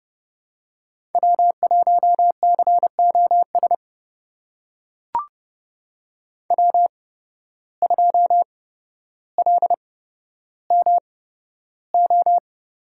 Morse code, heard as a keyed tone, about 30 wpm, spaced as for 13 wpm, 700 Hz. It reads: W1COS W2LMO
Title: USA Call Signs - 5 Characters - 8x Character Spacing - 30wpm